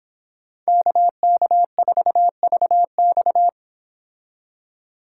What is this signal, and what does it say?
Morse code, heard as a keyed tone, about 26 wpm, 700 Hz. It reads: KK4VX